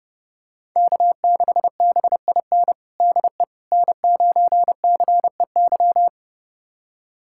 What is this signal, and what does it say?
Morse code, heard as a keyed tone, about 30 wpm, 700 Hz. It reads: K6BIN DE N9CEY
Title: USA Callsign DE Callsign - 5 Characters - 30wpm